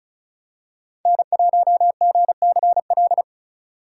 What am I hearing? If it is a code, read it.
Morse code, heard as a keyed tone, about 34 wpm, 700 Hz. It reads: N1GCL
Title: USA Call Signs - 5 Characters - 4x Character Spacing - 35wpm